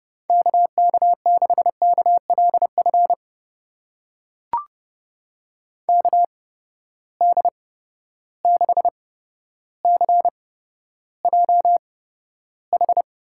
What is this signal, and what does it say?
Morse code, heard as a keyed tone, about 32 wpm, spaced as for 13 wpm, 700 Hz. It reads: KK6KLF KD6CJH
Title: USA Call Signs - 8x Character Spacing - 30wpm